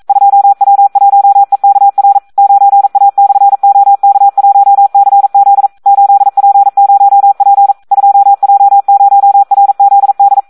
Morse code, heard as a keyed tone, about 42 wpm, 800 Hz, with radio noise. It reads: 2W1EKR 9A/OK1CZ 8P0P 2J0RZD
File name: Audio practica CW / SD 1 42